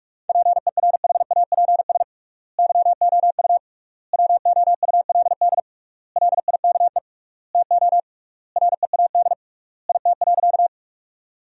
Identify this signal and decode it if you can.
Morse code, heard as a keyed tone, about 42 wpm, 700 Hz. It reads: PERHAPS YOU WOULD LIKE TO READ IT.
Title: 3x Word-Spacing - Sentences from Top 500 Words - 45wpm